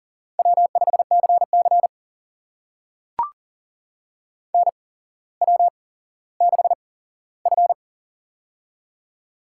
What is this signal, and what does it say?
Morse code, heard as a keyed tone, about 40 wpm, spaced as for 17 wpm, 700 Hz. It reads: W5CC NW6F
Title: USA Call Signs - 4 Characters - 8x Character Spacing - 40wpm